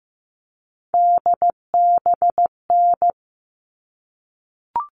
Morse code, heard as a keyed tone, about 15 wpm, 700 Hz. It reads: DBN
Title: All Cut Numbers - 3-Digit 15wpm